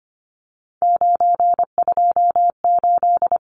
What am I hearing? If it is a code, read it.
Morse code, heard as a keyed tone, about 25 wpm, 700 Hz. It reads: 928
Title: Numbers - 3-Digit 25wpm